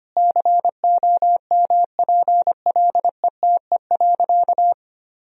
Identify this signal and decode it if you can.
Morse code, heard as a keyed tone, about 25 wpm, 700 Hz. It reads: COMPLETE.
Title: Sentences from Top 400 Words and Abbreviations 25wpm